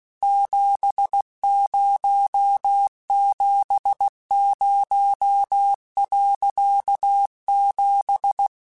Morse code, heard as a keyed tone, about 16 wpm, 800 Hz. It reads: 7070.7